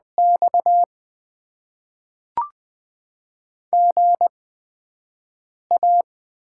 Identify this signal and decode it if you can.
Morse code, heard as a keyed tone, about 20 wpm, spaced as for 5 wpm, 700 Hz. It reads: X GA
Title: Top 20 QSO Elements - 8x Character Spacing - 20wpm